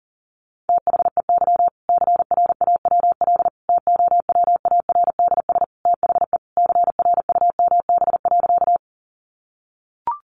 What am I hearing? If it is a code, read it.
Morse code, heard as a keyed tone, about 40 wpm, 700 Hz. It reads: THEY CRAWL TOWARDS THE CRUMB.
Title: Easy Phrases 40wpm